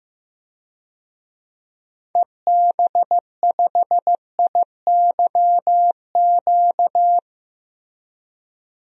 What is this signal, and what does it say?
Morse code, heard as a keyed tone, about 15 wpm, 700 Hz. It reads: EB5IYQ
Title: DXCC-100 Call Signs - 6 Characters - Repeated 2x - 15wpm